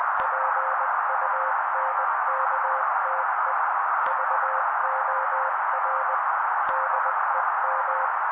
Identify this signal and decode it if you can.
Morse code, heard as a keyed tone, about 20 wpm, 530 Hz. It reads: PUNKTE VOR DEM
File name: daily / 18wpm noise